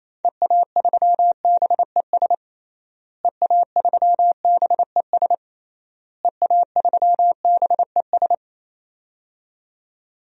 Morse code, heard as a keyed tone, about 28 wpm, 700 Hz. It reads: EA3BES EA3BES EA3BES
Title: DXCC-100 Call Signs - 6 Characters - Repeated 3x - 28wpm